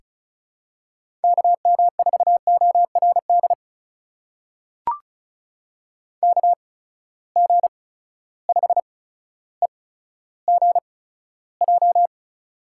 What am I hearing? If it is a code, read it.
Morse code, heard as a keyed tone, about 36 wpm, spaced as for 15 wpm, 700 Hz. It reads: KM4ORD KG5EGJ
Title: USA Call Signs - 8x Character Spacing - 35wpm